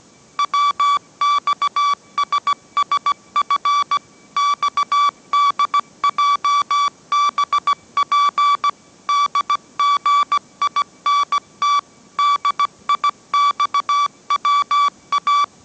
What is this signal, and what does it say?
Morse code, heard as a keyed tone, about 17 wpm, 1150 Hz. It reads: WXSSF XDJBP DGINT DIXWA
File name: mrx / mrx080 18wpm